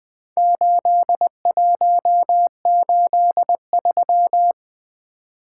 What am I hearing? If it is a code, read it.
Morse code, heard as a keyed tone, about 20 wpm, 700 Hz. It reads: 8183
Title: Four Letter-Number Combinations - 20wpm